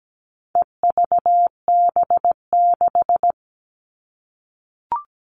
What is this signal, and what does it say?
Morse code, heard as a keyed tone, about 17 wpm, 700 Hz. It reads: EVB6
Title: All Cut Numbers - 4-Digit 17wpm